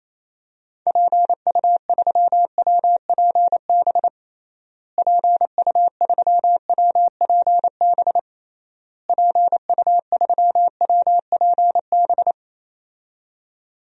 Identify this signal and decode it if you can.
Morse code, heard as a keyed tone, about 27 wpm, 700 Hz. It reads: PU3WPB PU3WPB PU3WPB
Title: DXCC-100 Call Signs - 6 Characters - Repeated 3x - 28wpm